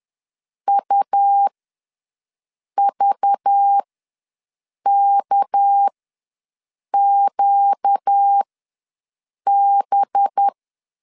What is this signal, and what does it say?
Morse code, heard as a keyed tone, about 10 wpm, spaced as for 6 wpm, 800 Hz. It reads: UVKQB
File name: LEZIONI / 6 PKX0 / 6 PK ws 10wpm 3745